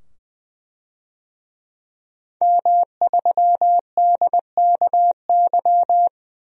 Morse code, heard as a keyed tone, about 20 wpm, 700 Hz. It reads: M3DKY
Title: DXCC-100 Call Signs - 5 Characters - 20wpm